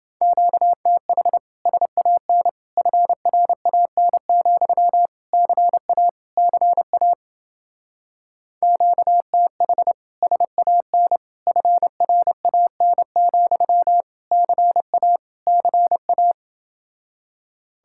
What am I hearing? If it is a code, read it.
Morse code, heard as a keyed tone, about 28 wpm, 700 Hz. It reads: QTH SAN FRAN, CA CA QTH SAN FRAN, CA CA